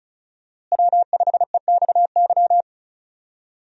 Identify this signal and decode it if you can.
Morse code, heard as a keyed tone, about 36 wpm, 700 Hz. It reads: W5EXY
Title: USA Call Signs - 5 Characters - Repeated 2x - 35wpm